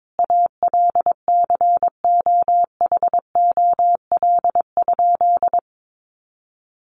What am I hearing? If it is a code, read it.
Morse code, heard as a keyed tone, about 22 wpm, 700 Hz. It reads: ALCOHOL?
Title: Sentences from Top 1400 Words 22wpm